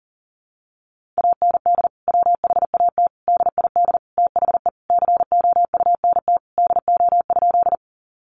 Morse code, heard as a keyed tone, about 40 wpm, 700 Hz. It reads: AND WHAT DID THE COUNT DO?
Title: Sentences from Top 1500 Words 40wpm